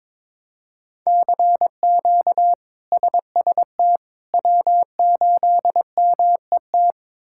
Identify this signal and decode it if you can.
Morse code, heard as a keyed tone, about 22 wpm, 700 Hz. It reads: CQ SST W8MET
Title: K1USN SST QSO Practice - 22wpm